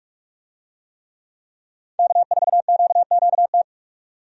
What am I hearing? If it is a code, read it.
Morse code, heard as a keyed tone, about 43 wpm, 700 Hz. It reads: K4QQT
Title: USA Call Signs - 5 Characters - 45wpm